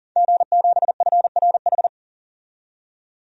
Morse code, heard as a keyed tone, about 38 wpm, 700 Hz. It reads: G7FRH
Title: DXCC-100 Call Signs - 5 Characters - 40wpm